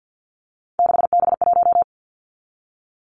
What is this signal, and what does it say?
Morse code, heard as a keyed tone, about 50 wpm, 700 Hz. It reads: 6B1
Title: Three Challenging Letter-Number Combinations 50wpm